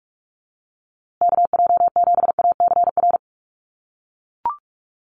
Morse code, heard as a keyed tone, about 44 wpm, 700 Hz. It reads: KJ7ACR
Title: USA Call Signs Repeated 2x 45wpm